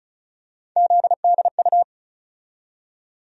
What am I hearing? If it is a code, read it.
Morse code, heard as a keyed tone, about 34 wpm, 700 Hz. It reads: ZDU
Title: USA Call Sign Postfixes - 3 Letters 35wpm